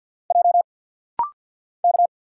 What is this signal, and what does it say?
Morse code, heard as a keyed tone, about 50 wpm, 700 Hz. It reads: J K
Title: Single Letters - Rapid-Fire 50wpm